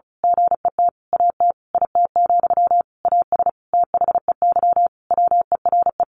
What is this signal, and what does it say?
Morse code, heard as a keyed tone, about 34 wpm, 700 Hz. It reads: GET AT IT, AS THEY WERE
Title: A Sound Of Thunder 35wpm